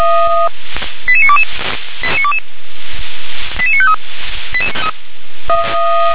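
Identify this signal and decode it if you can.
Five-tone selective call (ZVEI), heard as quick runs of five tones, tones of about 70 ms. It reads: 88922 88953
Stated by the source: Received on SDR, ZVEI code 88952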